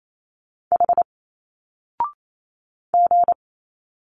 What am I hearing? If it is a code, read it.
Morse code, heard as a keyed tone, about 28 wpm, 700 Hz. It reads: H G
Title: Single Letters 28wpm